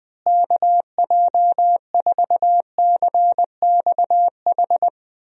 Morse code, heard as a keyed tone, about 20 wpm, 700 Hz. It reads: KJ4CXH
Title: USA Call Signs - 6 Characters - Repeated 3x - 20wpm